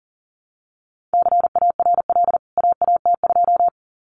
Morse code, heard as a keyed tone, about 40 wpm, 700 Hz. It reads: CARL AAT2
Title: CWT Exchange 40wpm